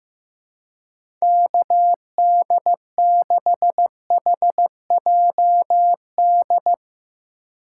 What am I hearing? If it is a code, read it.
Morse code, heard as a keyed tone, about 15 wpm, 700 Hz. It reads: KD6HJD
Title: USA Call Signs - 6 Characters - 3x Character Spacing - 15wpm